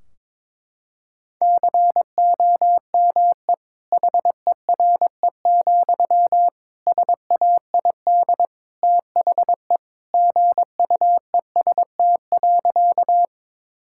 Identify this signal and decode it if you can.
Morse code, heard as a keyed tone, about 22 wpm, 700 Hz. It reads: COME HERE, SAID THE GUEST.
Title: Sentences from Top 1600 Words 22wpm